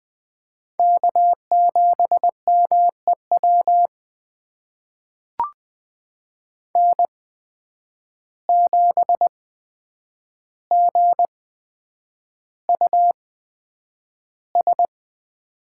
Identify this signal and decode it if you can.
Morse code, heard as a keyed tone, about 20 wpm, spaced as for 9 wpm, 700 Hz. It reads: K7MEW N7GUS
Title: USA Call Signs - 5 Characters - 8x Character Spacing - 20wpm